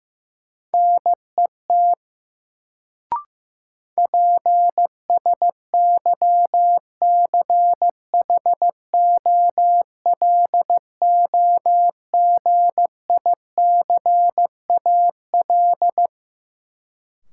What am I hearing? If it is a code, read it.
Morse code, heard as a keyed tone, about 15 wpm, 700 Hz. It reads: NET PSYCHOLOGICAL
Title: Sentences from Top 1400 Words - New Word Drill - 15wpm